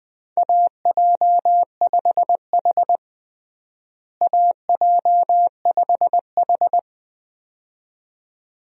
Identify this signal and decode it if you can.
Morse code, heard as a keyed tone, about 21 wpm, 700 Hz. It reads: AJ5H AJ5H
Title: USA Call Signs - 4 Characters - Repeated 2x - 20wpm